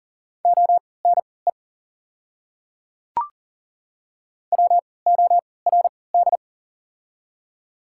Morse code, heard as a keyed tone, about 40 wpm, spaced as for 23 wpm, 690 Hz. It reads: ONE WORD
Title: 3x Character Spacing - Top 200 Words - 40wpm